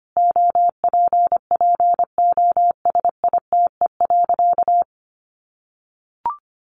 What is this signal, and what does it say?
Morse code, heard as a keyed tone, about 25 wpm, 700 Hz. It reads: OPPOSITE.
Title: Sentences from Top 1300 Words 25wpm